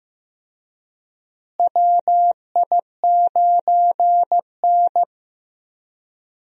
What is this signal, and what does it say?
Morse code, heard as a keyed tone, about 15 wpm, 700 Hz. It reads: WI9N
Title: USA Call Signs - 4 Characters - Repeated 3x - 15wpm